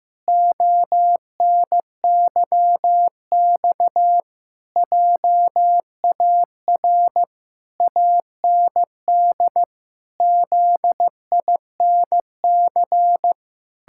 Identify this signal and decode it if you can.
Morse code, heard as a keyed tone, about 15 wpm, 700 Hz. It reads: ONYX JAR AND ZINC
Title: Pangram Challenge v2 - 15wpm